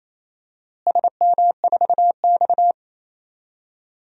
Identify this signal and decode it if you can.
Morse code, heard as a keyed tone, about 28 wpm, 700 Hz. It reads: SM4X
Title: DXCC-100 Call Signs - 4 Characters - Repeated 3x - 28wpm